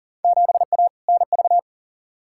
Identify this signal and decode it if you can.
Morse code, heard as a keyed tone, about 40 wpm, 700 Hz. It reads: 7A NV
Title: Field Day Exchanges Encore - 40wpm